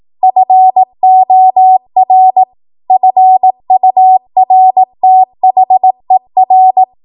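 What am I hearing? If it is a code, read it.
Morse code, heard as a keyed tone, about 18 wpm, 750 Hz. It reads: FOR FURTHER
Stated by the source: ARRL 18 wpm Morse code practice archive transmission.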